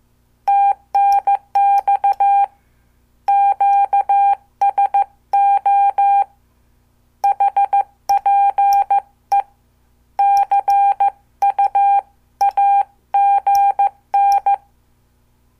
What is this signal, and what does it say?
Morse code, heard as a keyed tone, about 14 wpm, 800 Hz. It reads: TNX QSO HPE CUAGN
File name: morse / qso